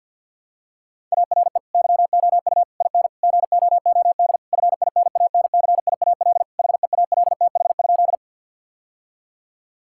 Morse code, heard as a keyed tone, about 50 wpm, 700 Hz. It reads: ARE YOU IN GOOD FINANCIAL HEALTH?
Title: Sentences from Top 1000 Words - Review Part 5 - 50wpm